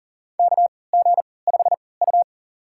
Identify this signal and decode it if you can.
Morse code, heard as a keyed tone, about 40 wpm, spaced as for 23 wpm, 700 Hz. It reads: KG5U
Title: USA Call Signs - 4 Characters - 3x Character Spacing - 40wpm